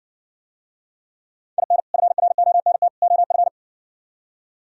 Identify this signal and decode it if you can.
Morse code, heard as a keyed tone, about 59 wpm, 700 Hz. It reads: IN FRONT OF